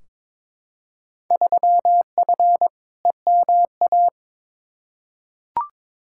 Morse code, heard as a keyed tone, about 22 wpm, 700 Hz. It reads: 3F EMA
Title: Field Day Exchanges Encore - 22wpm